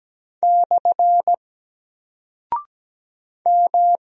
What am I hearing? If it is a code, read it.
Morse code, heard as a keyed tone, about 17 wpm, 700 Hz. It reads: / M
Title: Single Character (Letter:Number:Punctuation:Prosign) Alt 17wpm